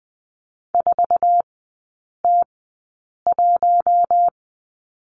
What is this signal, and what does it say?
Morse code, heard as a keyed tone, about 20 wpm, spaced as for 8 wpm, 700 Hz. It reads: 4T1
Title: 047 Lesson 9e - Number 4 - Three Characters 20wpm